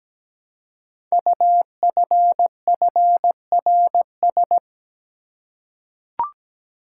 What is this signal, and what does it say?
Morse code, heard as a keyed tone, about 17 wpm, 700 Hz. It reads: UFFRS
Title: Code Groups - 5 Letters - 17wpm